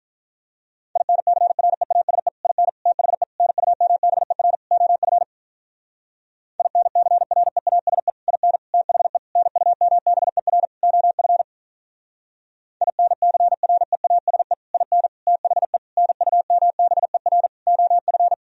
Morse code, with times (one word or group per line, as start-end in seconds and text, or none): 1.0-2.3 INCREASE
2.5-2.7 IN
2.9-3.2 THE
3.4-4.5 NUMBER
4.7-5.2 OF
6.6-8.1 INCREASE
8.3-8.6 IN
8.7-9.2 THE
9.4-10.7 NUMBER
10.8-11.4 OF
12.8-14.5 INCREASE
14.7-15.1 IN
15.3-15.8 THE
16.0-17.5 NUMBER
17.7-18.3 OF